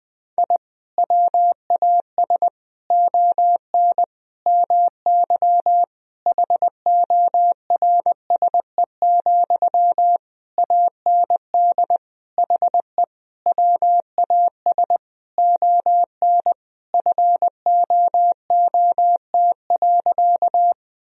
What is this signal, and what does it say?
Morse code, heard as a keyed tone, about 20 wpm, 700 Hz. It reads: I WAS ON MY HORSE, AND HE WAS ON FOOT.
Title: Sentences from Top 1000 Words - Review Part 3 - 20wpm